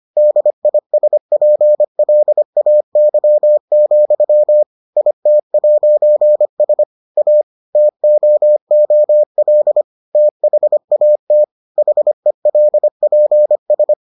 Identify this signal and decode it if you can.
Morse code, heard as a keyed tone, about 25 wpm, 600 Hz. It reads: DISPLAY, IT'S A TOOL THAT HELPS